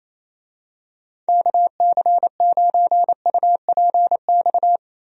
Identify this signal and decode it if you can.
Morse code, heard as a keyed tone, about 27 wpm, 700 Hz. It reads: KC9UPX
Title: USA Call Signs - 6 Characters - Repeated 2x - 28wpm